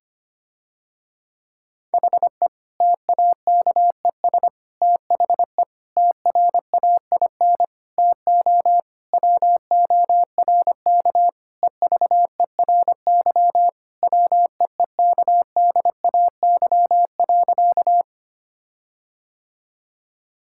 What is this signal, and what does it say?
Morse code, heard as a keyed tone, about 25 wpm, 700 Hz. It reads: HE TAKES THE TRAIN TO WORK EVERY WEEKDAY.
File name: practice / 25wpm / Difficult Sentences with Hints - 25wpm